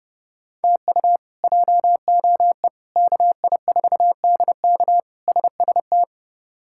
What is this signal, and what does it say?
Morse code, heard as a keyed tone, about 30 wpm, 700 Hz. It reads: TU JOE KI4DK SST
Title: K1USN SST QSO Practice - 30wpm